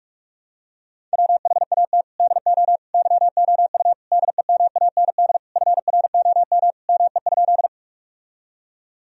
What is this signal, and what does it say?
Morse code, heard as a keyed tone, about 46 wpm, 700 Hz. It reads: WHAT DO YOU DEMAND FROM ME?
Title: Sentences from Top 1000 Words - Review Part 4 - 45wpm